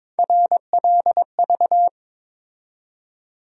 Morse code, heard as a keyed tone, about 22 wpm, 700 Hz. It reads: RLV
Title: USA Call Sign Postfixes - 3 Letters 22wpm